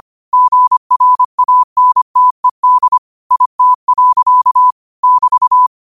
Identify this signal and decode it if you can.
Morse code, heard as a keyed tone, about 25 wpm, 1000 Hz. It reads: GRANTED IT. <BT>